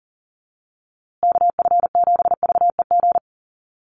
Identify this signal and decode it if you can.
Morse code, heard as a keyed tone, about 40 wpm, 700 Hz. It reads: KF7VEG
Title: USA Call Signs Repeated 2x 40wpm